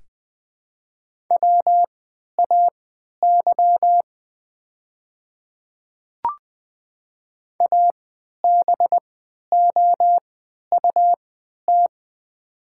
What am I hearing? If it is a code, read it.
Morse code, heard as a keyed tone, about 20 wpm, spaced as for 11 wpm, 700 Hz. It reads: WAY ABOUT
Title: 3x Character Spacing - Top 200 Words - 20wpm